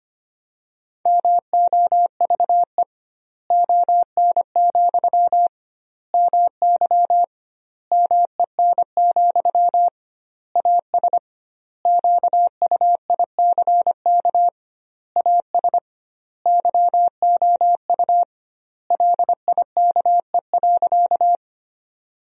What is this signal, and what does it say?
Morse code, heard as a keyed tone, about 25 wpm, 700 Hz. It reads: MOVE ON, MY MEN, AS QUICK AS YOU LIKE.
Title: 2x Word-Spacing - Sentences from Top 500 Words - 25wpm